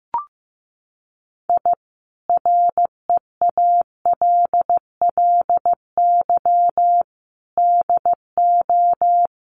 Morse code, heard as a keyed tone, about 15 wpm, 700 Hz. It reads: I REALLY DO
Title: Sets of 5 Words Encore 15wpm